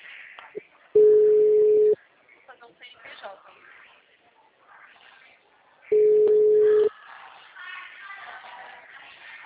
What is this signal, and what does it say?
Ringback tone (Europe, 425 Hz), with speech.
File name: out- -2013- - - .690